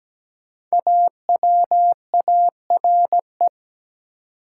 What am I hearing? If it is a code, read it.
Morse code, heard as a keyed tone, about 17 wpm, 700 Hz. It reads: AWARE
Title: Five Letter Words - 17wpm